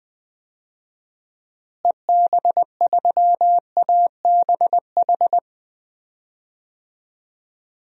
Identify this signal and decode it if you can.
Morse code, heard as a keyed tone, about 20 wpm, 700 Hz. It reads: EB3ABH
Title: DXCC-100 Call Signs - 6 Characters - Repeated 3x - 20wpm